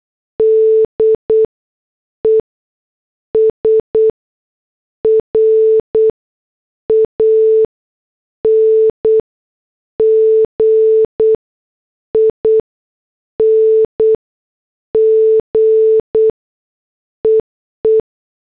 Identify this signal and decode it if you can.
Morse code, heard as a keyed tone, about 8 wpm, spaced as for 6 wpm, 440 Hz. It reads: DES RANGING EE
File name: morse / headlines